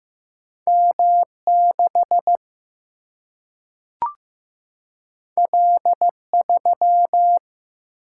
Morse code, heard as a keyed tone, about 15 wpm, 700 Hz. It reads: M 6 L 3
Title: Single-Letter Single-Number 15wpm